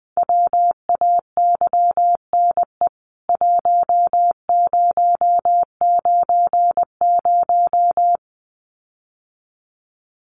Morse code, heard as a keyed tone, about 20 wpm, 700 Hz. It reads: WAYNE 1090
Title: CWT Exchange 20wpm